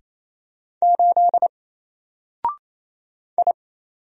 Morse code, heard as a keyed tone, about 28 wpm, 700 Hz. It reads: 8 I
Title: Single Letter-Number Alt 28wpm